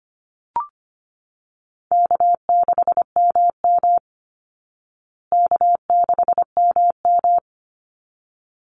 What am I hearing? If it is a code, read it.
Morse code, heard as a keyed tone, about 25 wpm, 700 Hz. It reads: K6MM K6MM
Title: Call Signs Repeated 3x - 25wpm